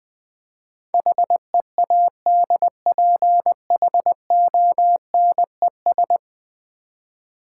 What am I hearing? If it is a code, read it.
Morse code, heard as a keyed tone, about 20 wpm, 700 Hz. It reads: HEADPHONES
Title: Easy Words with Hints - 20wpm